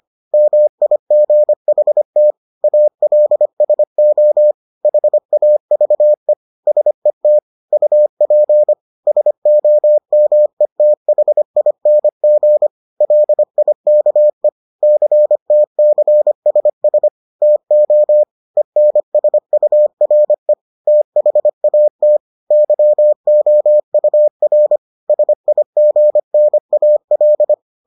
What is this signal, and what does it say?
Morse code, heard as a keyed tone, about 25 wpm, 600 Hz. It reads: MIGHT ALSO HAVE SET UP SOMETHING LIKE CTCSS TO ENSURE THAT YOUR SIGNAL